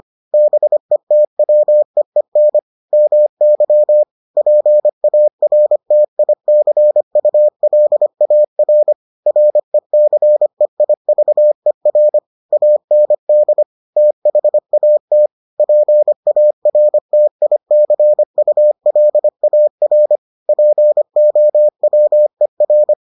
Morse code, heard as a keyed tone, about 25 wpm, 600 Hz. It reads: BETWEEN MY PARTICULAR RECEIVER AND THAT PARTICULAR POWER